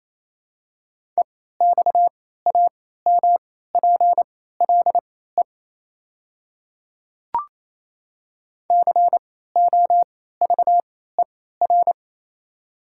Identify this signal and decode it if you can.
Morse code, heard as a keyed tone, about 28 wpm, spaced as for 16 wpm, 700 Hz. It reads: EXAMPLE COVER
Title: 3x Character Spacing - Top 300 Words - 28wpm